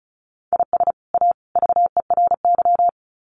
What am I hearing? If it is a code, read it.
Morse code, heard as a keyed tone, about 34 wpm, 700 Hz. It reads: IS A VERY
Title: Sets of 3 Words 35wpm